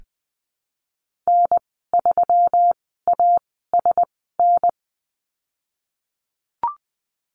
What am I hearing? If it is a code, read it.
Morse code, heard as a keyed tone, about 19 wpm, spaced as for 14 wpm, 700 Hz. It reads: N3ASN
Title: 207 Lesson 33g - Letter j - Call Signs 20wpm